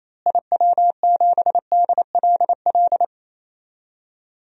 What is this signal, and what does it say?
Morse code, heard as a keyed tone, about 28 wpm, 700 Hz. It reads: IW7DLL
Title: DXCC-100 Call Signs - 6 Characters - Repeated 3x - 28wpm